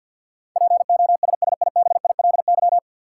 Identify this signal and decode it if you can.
Morse code, heard as a keyed tone, about 51 wpm, 700 Hz. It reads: POSSIBILY
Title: Sentences from Top 1300 Words - New Word Drill - 50wpm